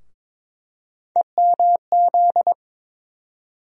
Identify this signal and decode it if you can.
Morse code, heard as a keyed tone, about 22 wpm, 700 Hz. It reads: EMZ
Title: USA Call Sign Postfixes - 3 Letters 22wpm